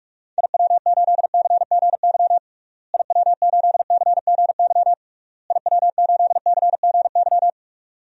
Morse code, heard as a keyed tone, about 46 wpm, 700 Hz. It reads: IW8CGY IW8CGY IW8CGY
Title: DXCC-100 Call Signs - 6 Characters - Repeated 3x - 45wpm